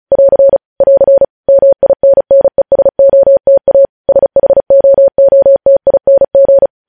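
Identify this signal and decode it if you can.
Morse code, heard as a keyed tone, about 36 wpm, 550 Hz. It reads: <AR> <AR> MINNESOTA SHOOTING
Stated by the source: Morse code news headlines sent at 35, beginning with vvv.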